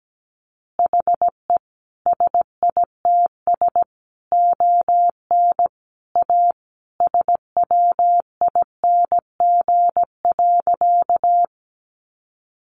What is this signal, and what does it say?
Morse code, heard as a keyed tone, about 17 wpm, 700 Hz. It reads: HE SITS ON A SWING.
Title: Additional Easy Phrases 17wpm